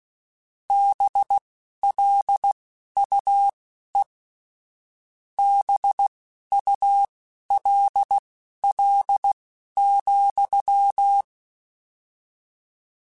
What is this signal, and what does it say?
Morse code, heard as a keyed tone, about 16 wpm, spaced as for 11 wpm, 800 Hz. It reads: BLUE BULL,
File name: Misc / Misc-mp3s / BIG @10